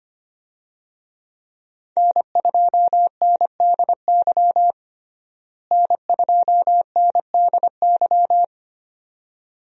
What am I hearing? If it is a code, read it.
Morse code, heard as a keyed tone, about 25 wpm, 700 Hz. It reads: N2NDY N2NDY
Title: USA Call Signs - 5 Characters - Repeated 2x - 25wpm